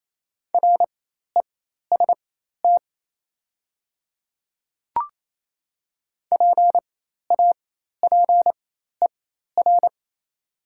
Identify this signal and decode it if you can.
Morse code, heard as a keyed tone, about 28 wpm, spaced as for 13 wpm, 690 Hz. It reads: REST PAPER
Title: 4x Character Spacing - Top 500 Words - 28wpm